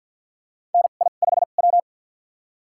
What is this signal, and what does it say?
Morse code, heard as a keyed tone, about 52 wpm, spaced as for 37 wpm, 700 Hz. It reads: NI5W
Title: USA Call Signs - 4 Characters - 2x Character Spacing - 50wpm